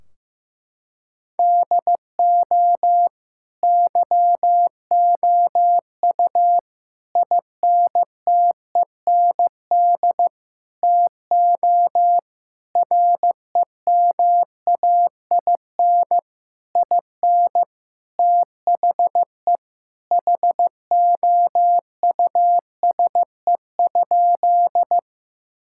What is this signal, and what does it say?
Morse code, heard as a keyed tone, about 15 wpm, 700 Hz. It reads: DO YOU INTEND TO REMAIN IN THE HOUSE?
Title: Sentences from Top 1600 Words 15wpm